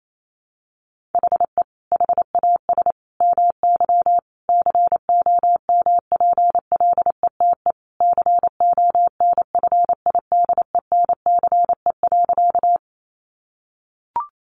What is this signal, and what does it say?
Morse code, heard as a keyed tone, about 28 wpm, 700 Hz. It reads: HE HAS MY COMPLETE CONFIDENCE.
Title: Sentences from Top 1500 Words 28wpm